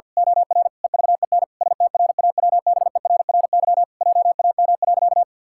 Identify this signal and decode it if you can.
Morse code, heard as a keyed tone, about 50 wpm, 700 Hz. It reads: OR EVEN STRAWBERRY JAM.
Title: Pangram Challenge v2 - 50wpm